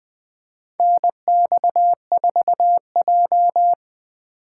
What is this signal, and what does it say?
Morse code, heard as a keyed tone, about 21 wpm, 700 Hz. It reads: NX4J
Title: USA Call Signs - 4 Characters - 3x Character Spacing - 20wpm